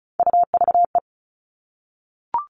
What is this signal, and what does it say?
Morse code, heard as a keyed tone, about 34 wpm, 700 Hz. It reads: UVE
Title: All Cut Numbers - 3-Digit 35wpm